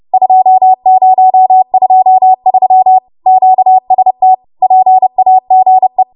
Morse code, heard as a keyed tone, about 30 wpm, 750 Hz. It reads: 2023 QST PAGE
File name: Morse / 30WPM